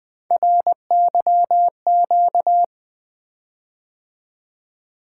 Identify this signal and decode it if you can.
Morse code, heard as a keyed tone, about 20 wpm, 700 Hz. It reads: RYQ
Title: Three Letter-Number Combinations - 20wpm